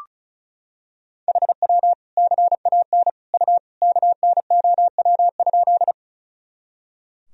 Morse code, heard as a keyed tone, about 36 wpm, 700 Hz. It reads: HW CAN U KNOW?
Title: Sentences from Top 100 Words and Abbreviations 35wpm